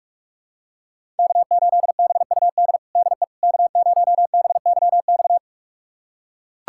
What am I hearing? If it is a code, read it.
Morse code, heard as a keyed tone, about 41 wpm, 700 Hz. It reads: K8BUD DE K0BYX
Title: USA Callsign DE Callsign - 5 Characters - 45wpm